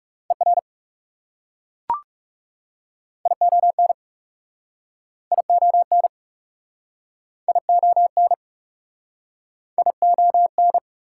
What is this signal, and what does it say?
Morse code, heard as a keyed tone, about 38 wpm, 700 Hz. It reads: ER ION ION ION ION
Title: Speed-Racing Prefixes and Suffixes 30wpm